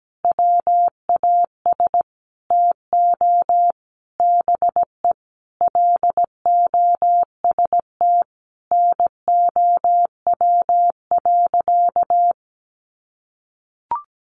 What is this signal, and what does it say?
Morse code, heard as a keyed tone, about 17 wpm, 700 Hz. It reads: WAS TO BE LOST NOW.
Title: Sentences from Top 1400 Words 17wpm